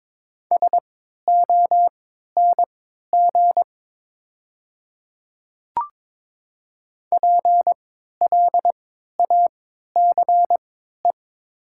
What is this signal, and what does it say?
Morse code, heard as a keyed tone, about 22 wpm, spaced as for 13 wpm, 700 Hz. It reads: SONG PLACE
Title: 3x Character Spacing - Top 500 Words - 22wpm